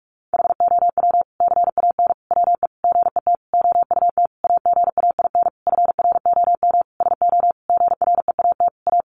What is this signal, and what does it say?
Morse code, heard as a keyed tone, about 44 wpm, 700 Hz. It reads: HOW CAN WE GET OUT AGAIN FROM SO GREAT A
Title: Sentences from Top 1200 Words 45wpm